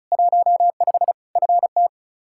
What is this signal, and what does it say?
Morse code, heard as a keyed tone, about 36 wpm, 700 Hz. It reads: 15 FT
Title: Practice QSO - Exchange 2 - Info - 35wpm